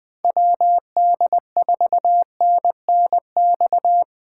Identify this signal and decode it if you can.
Morse code, heard as a keyed tone, about 20 wpm, 700 Hz. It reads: WD4NNX
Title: USA Call Signs - 4x Character Spacing - 20wpm